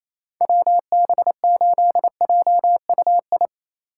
Morse code, heard as a keyed tone, about 27 wpm, 700 Hz. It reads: WB8JUI
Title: K1USN SST QSO Practice - 28wpm